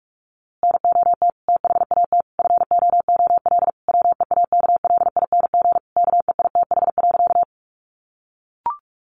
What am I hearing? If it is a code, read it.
Morse code, heard as a keyed tone, about 44 wpm, 700 Hz. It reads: NOT THAT FOOL WEAKLING KEITH.
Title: A Sound Of Thunder 45wpm